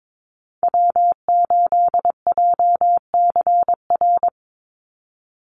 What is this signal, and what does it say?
Morse code, heard as a keyed tone, about 22 wpm, 700 Hz. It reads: W8JCR
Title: USA Call Signs 22wpm